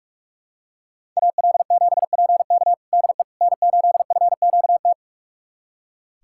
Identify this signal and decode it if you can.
Morse code, heard as a keyed tone, about 46 wpm, 700 Hz. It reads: AL7PK DE N8FQT
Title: USA Callsign DE Callsign - 5 Characters - 45wpm